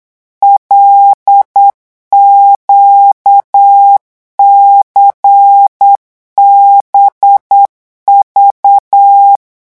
Morse code, heard as a keyed tone, about 8 wpm, 800 Hz. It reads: LQCBV
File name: morse mp3 / revision8-vitesse 8 mots